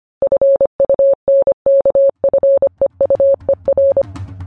Morse code, heard as a keyed tone, about 25 wpm, 550 Hz. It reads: FUNKFEUER